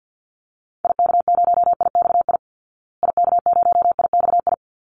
Morse code, heard as a keyed tone, about 50 wpm, 700 Hz. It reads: IK0IXI IK0IXI
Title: Call Signs Repeated 3x - 50wpm